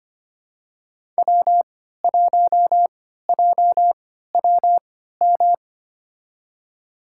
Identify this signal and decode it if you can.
Morse code, heard as a keyed tone, about 25 wpm, spaced as for 14 wpm, 700 Hz. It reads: W1JWM
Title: USA Call Signs - 5 Characters - 3x Character Spacing - 25wpm